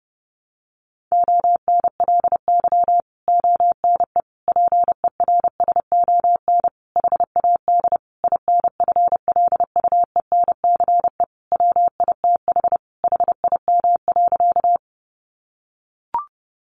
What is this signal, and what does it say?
Morse code, heard as a keyed tone, about 30 wpm, 700 Hz. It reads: ONLY ONE PERSON HAD INFLUENCE WITH HIM.
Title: Sentences from Top 1100 Words 30wpm